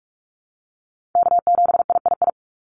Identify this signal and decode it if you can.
Morse code, heard as a keyed tone, about 46 wpm, 700 Hz. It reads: K7III
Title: NA SOTA Chaser Callsigns 45wpm